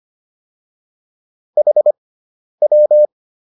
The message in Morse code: HW

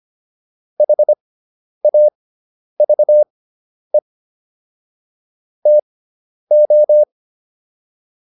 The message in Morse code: HAVE TO

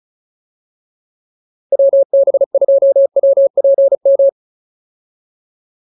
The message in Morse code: WB2WPM

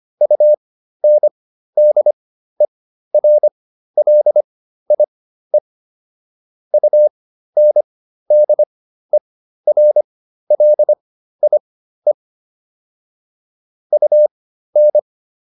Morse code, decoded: UNDERLIE UNDERLIE UN

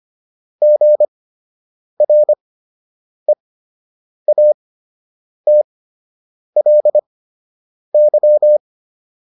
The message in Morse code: GREATLY

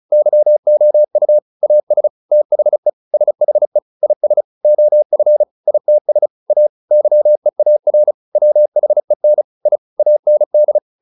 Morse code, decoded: YOU AS THE SHE IS OF ITS A YEAR WHEN I AND

